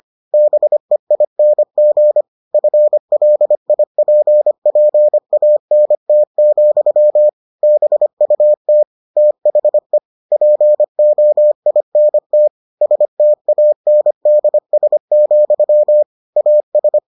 Morse code, heard as a keyed tone, about 25 wpm, 600 Hz. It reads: BEING FLIPPANT, BUT THE POINT STANDS, AS